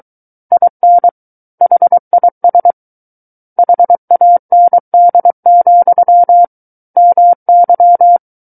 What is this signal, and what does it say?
Morse code, heard as a keyed tone, about 23 wpm, 700 Hz. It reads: IN HIS HAND, MY
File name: Book One / 14-Chapter 14 - 23 WPM